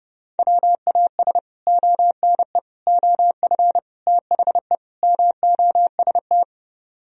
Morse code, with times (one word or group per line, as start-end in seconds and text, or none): 0.4-1.4 WAS
1.7-2.6 ONE
2.9-3.8 OF
4.1-4.8 THE
5.0-6.4 MOST